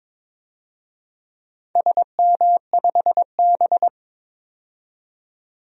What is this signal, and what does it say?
Morse code, heard as a keyed tone, about 22 wpm, 700 Hz. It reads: SM5B
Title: DXCC-100 Call Signs - 4 Characters - 22wpm